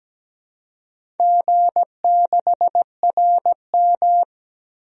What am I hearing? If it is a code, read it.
Morse code, heard as a keyed tone, about 17 wpm, 700 Hz. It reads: G6RM
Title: DXCC-100 Call Signs - 4 Characters - 17wpm